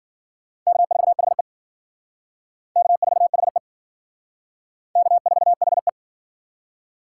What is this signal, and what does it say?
Morse code, heard as a keyed tone, about 60 wpm, 700 Hz. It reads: K4HE K4HE K4HE